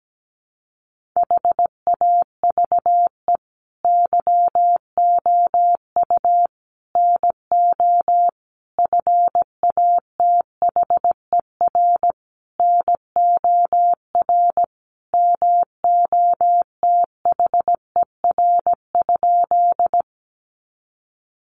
Morse code, heard as a keyed tone, about 17 wpm, 700 Hz. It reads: HAVE YOU NO FATHER NOR MOTHER?
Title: Sentences from Top 1500 Words 17wpm